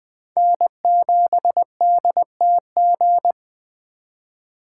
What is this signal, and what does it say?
Morse code, heard as a keyed tone, about 21 wpm, 700 Hz. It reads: N7DTG
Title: Speed Racing - USA Call Signs 17wpm